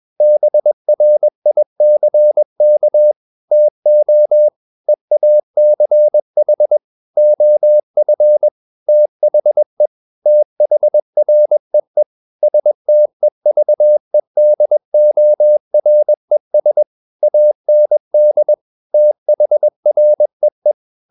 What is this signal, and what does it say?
Morse code, heard as a keyed tone, about 21 wpm, 600 Hz. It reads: BRICK TO EACH OF THE THREE STEVEDORES AND THREE
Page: CW med Gnister